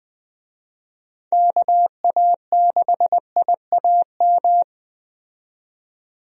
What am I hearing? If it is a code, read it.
Morse code, heard as a keyed tone, about 20 wpm, 700 Hz. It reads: KA6IAM
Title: USA Call Signs - 6 Characters - 20wpm